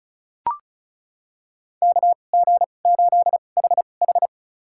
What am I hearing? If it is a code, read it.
Morse code, heard as a keyed tone, about 34 wpm, spaced as for 25 wpm, 700 Hz. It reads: KG8HH
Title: USA Call Signs - 5 Characters - 2x Character Spacing - 35wpm